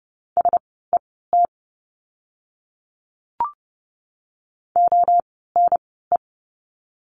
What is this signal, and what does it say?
Morse code, heard as a keyed tone, about 30 wpm, spaced as for 17 wpm, 700 Hz. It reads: SET ONE
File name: Lesson 16f - Letter c - Words 30wpm